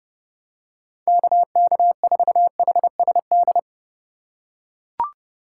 Morse code, heard as a keyed tone, about 30 wpm, 700 Hz. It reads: KK4HSD
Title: USA Call Signs - 6 Characters - 2x Character Spacing - 30wpm